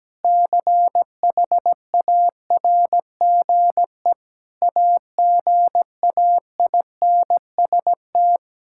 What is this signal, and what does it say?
Morse code, heard as a keyed tone, about 17 wpm, 700 Hz. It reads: CHARGE AGAINST
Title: Sentences from Top 1000 Words - Review Part 6 - 17wpm